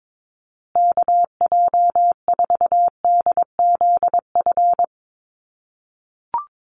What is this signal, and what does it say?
Morse code, heard as a keyed tone, about 22 wpm, 700 Hz. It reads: KJ4DZF
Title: USA Call Signs Repeated 2x 22wpm